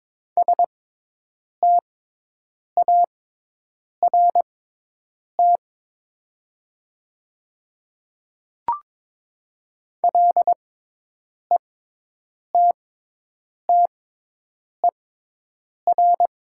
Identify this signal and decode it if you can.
Morse code, heard as a keyed tone, about 21 wpm, spaced as for 8 wpm, 700 Hz. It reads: START LETTER
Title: 6x Character Spacing - Top 300 Words - 22wpm